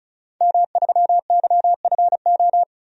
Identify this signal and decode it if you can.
Morse code, heard as a keyed tone, about 34 wpm, 700 Hz. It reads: M3YFO
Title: DXCC-100 Call Signs - 5 Characters - 35wpm